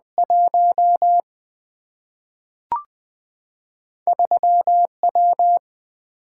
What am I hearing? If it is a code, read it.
Morse code, heard as a keyed tone, about 20 wpm, 700 Hz. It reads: 1 3 W
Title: Single-Letter Single-Number 20wpm